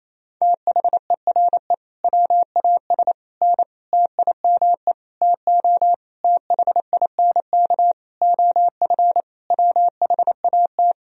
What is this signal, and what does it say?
Morse code, heard as a keyed tone, about 28 wpm, 700 Hz. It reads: THERE WAS N TIME TO THINK OF WHAT
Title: Sentences from Top 400 Words and Abbreviations 28wpm